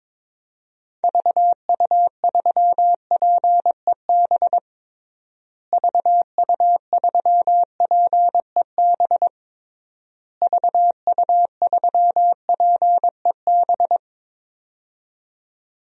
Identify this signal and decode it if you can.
Morse code, heard as a keyed tone, about 22 wpm, 700 Hz. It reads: VU3PEB VU3PEB VU3PEB
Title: DXCC-100 Call Signs - 6 Characters - Repeated 3x - 22wpm